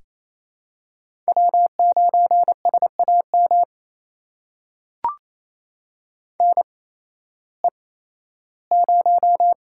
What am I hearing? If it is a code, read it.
Morse code, heard as a keyed tone, about 28 wpm, spaced as for 15 wpm, 700 Hz. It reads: W9SAM NE0
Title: USA Call Signs - 5 Characters - 8x Character Spacing - 28wpm